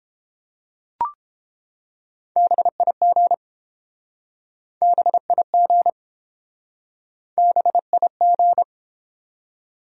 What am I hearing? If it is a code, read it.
Morse code, heard as a keyed tone, about 31 wpm, 700 Hz. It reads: BIG BIG BIG